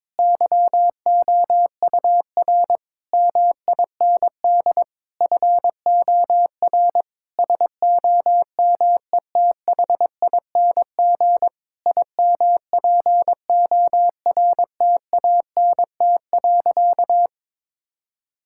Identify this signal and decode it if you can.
Morse code, heard as a keyed tone, about 22 wpm, 700 Hz. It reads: YOUR MIND FOR SOMETHING IMPORTANT.
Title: Sentences from Top 1000 Words - Review Part 6 - 22wpm